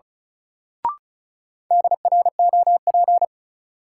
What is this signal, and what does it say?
Morse code, heard as a keyed tone, about 34 wpm, 700 Hz. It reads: DROP
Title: Sentences from Top 1400 Words - New Word Drill - 35wpm